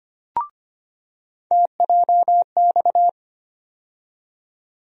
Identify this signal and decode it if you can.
Morse code, heard as a keyed tone, about 25 wpm, 700 Hz. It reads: TJX
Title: USA Call Sign Postfixes - 3 Letters 25wpm